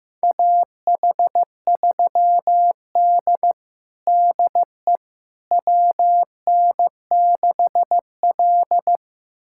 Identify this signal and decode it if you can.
Morse code, heard as a keyed tone, about 15 wpm, 700 Hz. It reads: AH3D DE WN6L
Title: USA Callsign DE Callsign - 4 Characters - 15wpm